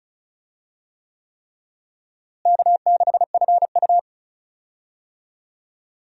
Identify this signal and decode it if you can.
Morse code, heard as a keyed tone, about 36 wpm, 700 Hz. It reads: K6FU
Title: USA Call Signs - 4 Characters - Repeated 3x - 35wpm